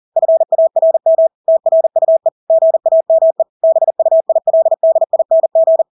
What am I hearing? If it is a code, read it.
Morse code, heard as a keyed tone, about 40 wpm, 650 Hz. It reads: FARM TRUE GAME BUILDING
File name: Engelske ord 40-40 wpm